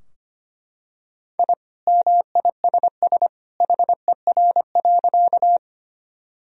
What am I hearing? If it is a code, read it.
Morse code, heard as a keyed tone, about 25 wpm, 700 Hz. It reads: I MISS HER.
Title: Sentences from Top 1000 Words - Review Part 3 - 25wpm